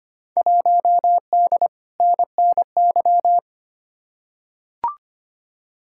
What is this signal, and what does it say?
Morse code, heard as a keyed tone, about 25 wpm, 700 Hz. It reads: 1D NNY
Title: Field Day Exchanges - 25wpm